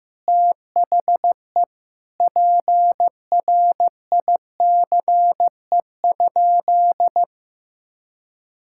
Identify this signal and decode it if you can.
Morse code, heard as a keyed tone, about 15 wpm, 700 Hz. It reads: THE PRICE?
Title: Sentences from Top 1600 Words 15wpm